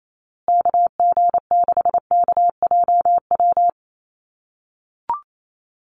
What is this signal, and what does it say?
Morse code, heard as a keyed tone, about 28 wpm, 700 Hz. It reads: KG6KJW
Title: USA Call Signs Repeated 2x 28wpm